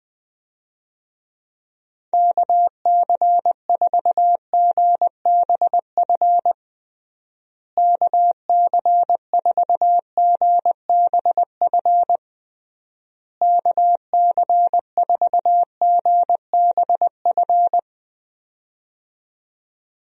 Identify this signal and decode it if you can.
Morse code, heard as a keyed tone, about 20 wpm, 700 Hz. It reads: KC4GBF KC4GBF KC4GBF
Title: USA Call Signs - 6 Characters - Repeated 3x - 20wpm